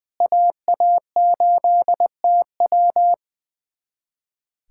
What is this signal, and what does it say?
Morse code, heard as a keyed tone, about 20 wpm, 700 Hz. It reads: AA8TW
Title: USA Callsign DE Callsign 20wpm